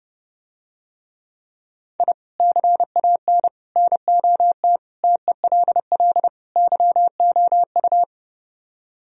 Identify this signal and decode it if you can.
Morse code, heard as a keyed tone, about 32 wpm, 700 Hz. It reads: I CAN NOT TELL YOU